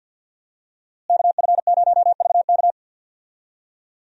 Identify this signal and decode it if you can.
Morse code, heard as a keyed tone, about 52 wpm, 700 Hz. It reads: KF0VK
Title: USA Call Signs - 5 Characters - Repeated 3x - 50wpm